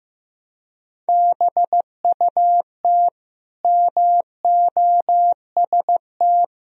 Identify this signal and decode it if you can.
Morse code, heard as a keyed tone, about 15 wpm, 700 Hz. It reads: BUT MOST
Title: Sentences from Top 1000 Words - Review Part 4 - 15wpm